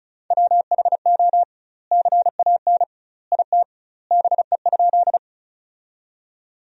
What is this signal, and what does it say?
Morse code, heard as a keyed tone, about 36 wpm, 700 Hz. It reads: WHO CAN IT BE?
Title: 2x Word-Spacing - Sentences from Top 100 Words - 35wpm